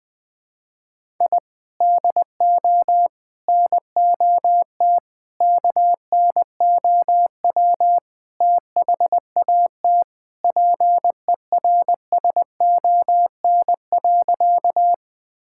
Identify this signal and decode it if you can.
Morse code, heard as a keyed tone, about 21 wpm, 700 Hz. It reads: I DO NOT KNOW THAT PERSON.
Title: Sentences with 5-Word Phrases - Spoken Beginning - 20wpm